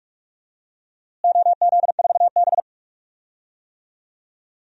Morse code, heard as a keyed tone, about 42 wpm, 700 Hz. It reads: OZ4B
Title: DXCC-100 Call Signs - 4 Characters - Repeated 2x - 45wpm